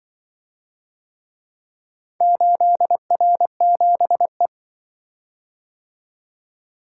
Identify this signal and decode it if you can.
Morse code, heard as a keyed tone, about 24 wpm, 700 Hz. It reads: 8R7E
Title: Speed-Racing - Four Letter-Number Combinations - 20wpm